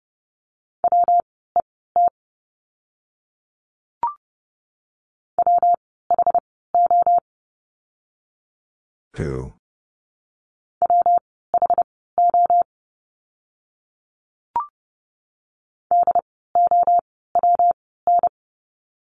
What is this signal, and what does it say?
Morse code, heard as a keyed tone, about 30 wpm, spaced as for 17 wpm, 700 Hz. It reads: WET WHO WHO DOWN
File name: course / 30:10wpm / 115 Lesson 19f - Letter w - Words 30wpm